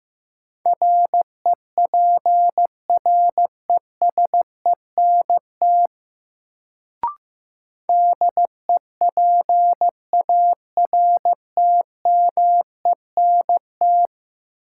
Morse code, heard as a keyed tone, about 16 wpm, 700 Hz. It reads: REPRESENT DEPARTMENT
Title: Sentences from Top 1200 Words - New Word Drill Alt - 15wpm